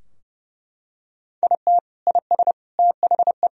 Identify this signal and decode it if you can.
Morse code, heard as a keyed tone, about 30 wpm, 700 Hz. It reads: IT IS THE
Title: Sentences from Top 1000 Words - Review Part 2 - 30wpm